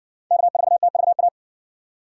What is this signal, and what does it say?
Morse code, heard as a keyed tone, about 60 wpm, 700 Hz. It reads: K4TVA